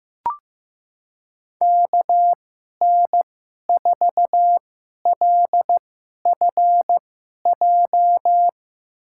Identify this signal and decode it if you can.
Morse code, heard as a keyed tone, about 15 wpm, spaced as for 11 wpm, 700 Hz. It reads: KN4LFJ
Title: USA Call Signs - 6 Characters - 2x Character Spacing - 15wpm